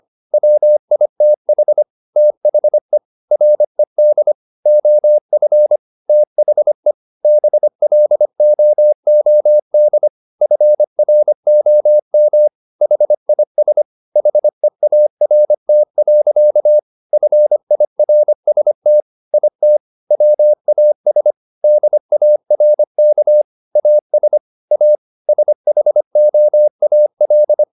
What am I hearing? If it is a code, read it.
Morse code, heard as a keyed tone, about 25 wpm, 600 Hz. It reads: WITH THE RED OF THE BLOOD FROM HIS HEART. FIRST IT WAS DARK AS A SHOAL